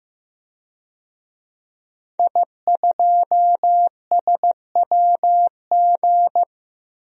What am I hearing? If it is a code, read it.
Morse code, heard as a keyed tone, about 15 wpm, 700 Hz. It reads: I2SWG
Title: DXCC-100 Call Signs - 5 Characters - 15wpm